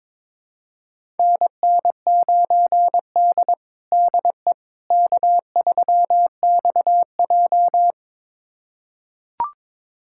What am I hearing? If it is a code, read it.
Morse code, heard as a keyed tone, about 22 wpm, 700 Hz. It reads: NN9D DE K3XJ
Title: USA Callsign DE Callsign - 4 Characters - 22wpm